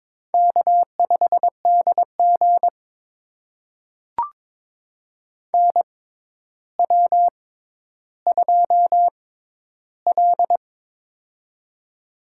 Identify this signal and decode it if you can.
Morse code, heard as a keyed tone, about 22 wpm, spaced as for 11 wpm, 700 Hz. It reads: K5DG NW2L
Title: USA Call Signs - 6x Character Spacing - 22wpm